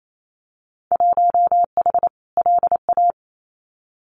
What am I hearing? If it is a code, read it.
Morse code, heard as a keyed tone, about 28 wpm, 700 Hz. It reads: 1H LA
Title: Winter Field Day Exchanges 28wpm